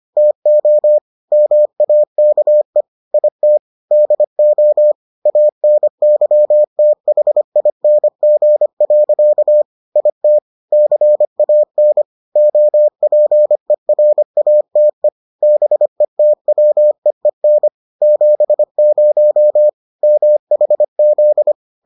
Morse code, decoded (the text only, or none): TO MAKE IT DO ANYTHING. IT CAN OPERATE BETWEEN 70 MHZ